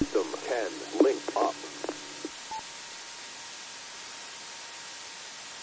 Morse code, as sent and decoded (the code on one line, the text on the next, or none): none